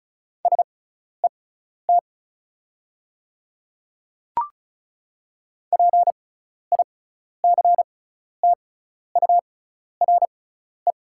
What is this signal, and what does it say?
Morse code, heard as a keyed tone, about 36 wpm, spaced as for 12 wpm, 700 Hz. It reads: SET PICTURE
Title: 6x Character Spacing - Top 200 Words - 35wpm